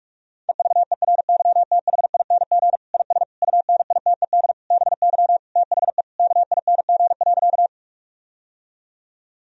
Morse code, heard as a keyed tone, about 46 wpm, 700 Hz. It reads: EVERYTHING IS UNITED BY THE KING.
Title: Sentences from Top 1000 Words - Review Part 4 - 45wpm